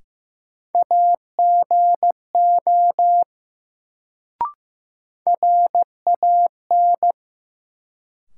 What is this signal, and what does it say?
Morse code, heard as a keyed tone, about 15 wpm, 700 Hz. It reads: AGO RAN
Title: Sentences from Top 500 Words - New Word Drill - 15wpm